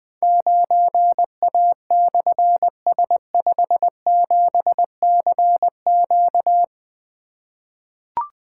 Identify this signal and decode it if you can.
Morse code, heard as a keyed tone, about 21 wpm, 700 Hz. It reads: 9A/S57CQ
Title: Difficult Call Signs 20wpm